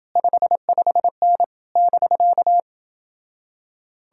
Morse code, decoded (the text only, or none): 55N <BK>